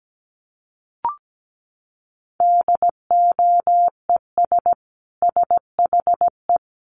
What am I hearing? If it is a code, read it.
Morse code, heard as a keyed tone, about 17 wpm, 700 Hz. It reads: DOES SHE
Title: Sentences from Top 1100 Words 17wpm